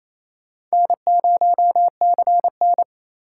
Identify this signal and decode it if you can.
Morse code, heard as a keyed tone, about 32 wpm, 700 Hz. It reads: N0CN
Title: Speed Racing - USA 1x2 Call Signs 28wpm